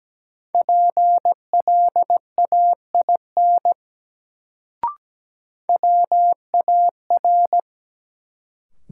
Morse code, heard as a keyed tone, about 17 wpm, 700 Hz. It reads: PLAIN WAR
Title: Sentences from Top 400 Words - New Word Drill - 17wpm